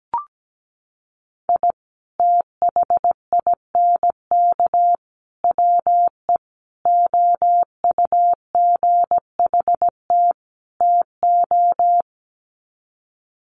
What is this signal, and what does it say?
Morse code, heard as a keyed tone, about 17 wpm, 700 Hz. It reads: I THINK WE OUGHT TO
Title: Sets of 5 Words Encore 17wpm